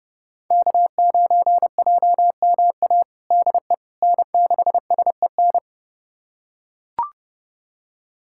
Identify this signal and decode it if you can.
Morse code, heard as a keyed tone, about 30 wpm, 700 Hz. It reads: K9JMA DE N6SEN
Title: USA Callsign DE Callsign - 5 Characters - 30wpm